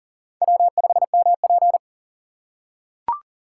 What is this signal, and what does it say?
Morse code, heard as a keyed tone, about 40 wpm, 700 Hz. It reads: W5MP
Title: USA Call Signs - 6x Character Spacing - 40wpm